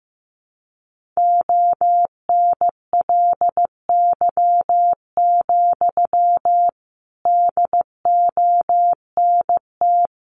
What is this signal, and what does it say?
Morse code, heard as a keyed tone, about 14 wpm, 700 Hz. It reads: ONLY, DONT
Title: Sentences from Top 300 Words 15wpm